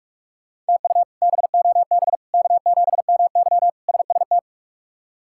Morse code, heard as a keyed tone, about 46 wpm, 700 Hz. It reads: TU BOB K7MY SST
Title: K1USN SST QSO Practice - 45wpm